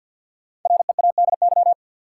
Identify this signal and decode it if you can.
Morse code, heard as a keyed tone, about 49 wpm, 700 Hz. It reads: READY
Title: Five Letter Words - 50wpm